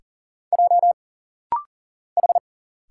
Morse code, heard as a keyed tone, about 41 wpm, 700 Hz. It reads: J H
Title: Single Letter-Number Alt - Rapid-Fire 40wpm